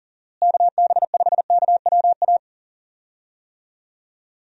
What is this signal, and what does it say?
Morse code, heard as a keyed tone, about 40 wpm, 700 Hz. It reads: KB5KWA
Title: USA Call Signs - 6 Characters - Repeated 3x - 40wpm